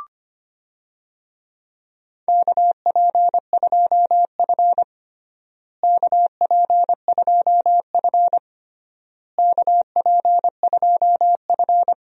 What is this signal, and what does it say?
Morse code, heard as a keyed tone, about 25 wpm, 700 Hz. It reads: KP2F KP2F KP2F
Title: DXCC-100 Call Signs - 4 Characters - Repeated 3x - 25wpm